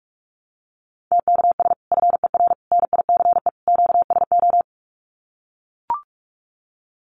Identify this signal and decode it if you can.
Morse code, heard as a keyed tone, about 44 wpm, 700 Hz. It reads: TKS FER NICE QSO
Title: Common QSO Phrases 45wpm